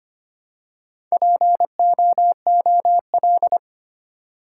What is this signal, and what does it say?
Morse code, heard as a keyed tone, about 25 wpm, 700 Hz. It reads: POOL
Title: Four Letter Words - 25wpm